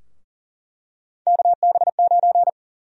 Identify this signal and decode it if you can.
Morse code, heard as a keyed tone, about 40 wpm, 700 Hz. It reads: KB9
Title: USA Call Sign Prefixes - Repeated 2x - 40wpm